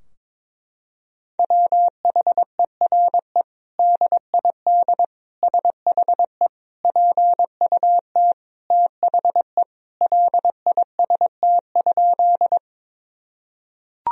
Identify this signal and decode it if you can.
Morse code, heard as a keyed tone, about 22 wpm, 700 Hz. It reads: WHERE DID SHE PUT THE LIST?
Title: Sentences from Top 1000 Words - Review Part 2 - 22wpm